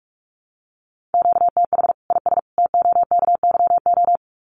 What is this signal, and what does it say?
Morse code, heard as a keyed tone, about 44 wpm, 700 Hz. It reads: QTH IS TOKYO
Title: Common QSO Phrases 45wpm